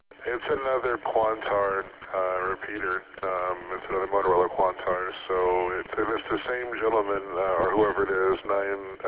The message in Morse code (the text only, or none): none